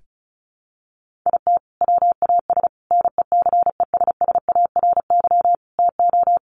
Morse code, heard as a keyed tone, about 34 wpm, 700 Hz. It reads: IT WAS NECESSARY TO
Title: Sentences from Top 1400 Words 35wpm